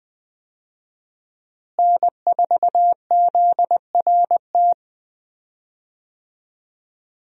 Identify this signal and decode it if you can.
Morse code, heard as a keyed tone, about 20 wpm, 700 Hz. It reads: N4ZRT
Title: USA Call Signs - 5 Characters - Repeated 3x - 20wpm